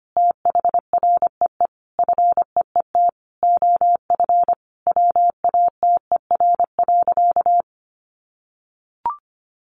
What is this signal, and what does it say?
Morse code, heard as a keyed tone, about 25 wpm, 700 Hz. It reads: THREE FEET OF WATER.
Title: Sentences from Top 700 Words 25wpm